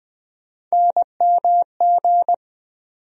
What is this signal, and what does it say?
Morse code, heard as a keyed tone, about 20 wpm, 700 Hz. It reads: NMG
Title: USA Call Sign Postfixes - 3 Letters 20wpm